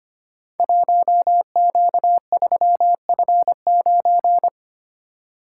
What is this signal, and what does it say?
Morse code, heard as a keyed tone, about 25 wpm, 700 Hz. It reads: 1Q3F9
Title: Code Groups - 5 Letter:Numbers - 25wpm